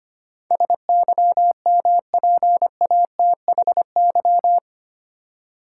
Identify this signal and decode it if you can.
Morse code, heard as a keyed tone, about 25 wpm, 700 Hz. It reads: SYMPATHY
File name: 8 Letter Words - First 3 Letters Given - 25wpm